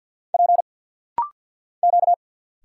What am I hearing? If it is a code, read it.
Morse code, heard as a keyed tone, about 50 wpm, 700 Hz. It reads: P Q
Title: Single Letters - Rapid-Fire 50wpm